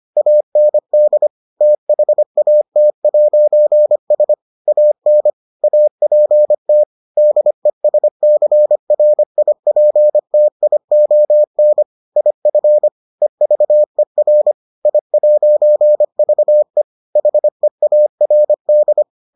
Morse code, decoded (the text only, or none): AND THAT'S AN APT DESCRIPTION IF EVER I'VE HEARD